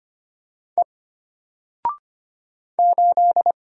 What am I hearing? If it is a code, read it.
Morse code, heard as a keyed tone, about 25 wpm, 700 Hz. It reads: E 8
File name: Single-Letter-Number-25wpm